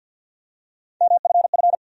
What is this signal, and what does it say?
Morse code, heard as a keyed tone, about 52 wpm, 700 Hz. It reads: MFF
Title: USA Call Sign Postfixes - 3 Letters 50wpm